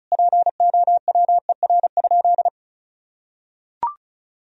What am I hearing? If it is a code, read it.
Morse code, heard as a keyed tone, about 34 wpm, 700 Hz. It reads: POWER?
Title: Sentences from Top 1000 Words - Review Part 2 - 35wpm